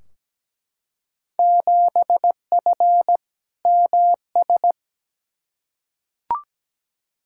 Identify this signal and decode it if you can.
Morse code, heard as a keyed tone, about 17 wpm, 700 Hz. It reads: 7F MS
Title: Field Day Exchanges Encore - 17wpm